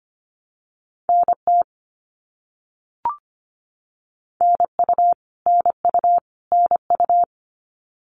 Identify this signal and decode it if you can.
Morse code, heard as a keyed tone, about 25 wpm, 700 Hz. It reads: NT NU NU NU
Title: Canadian Provinces and Territories - Repeated 3x - 25wpm